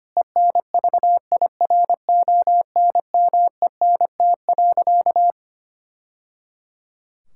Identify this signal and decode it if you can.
Morse code, heard as a keyed tone, about 25 wpm, 700 Hz. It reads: ENVIRONMENT.
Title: Sentences from Top 1000 Words - Review Part 5 - 25wpm